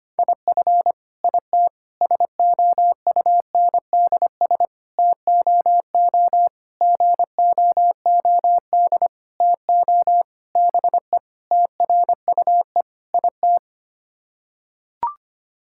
Morse code, heard as a keyed tone, about 25 wpm, 700 Hz. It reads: IF IT SOUNDS TOO GOOD TO BE TRUE IT
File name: Sets of 9 Words - 25wpm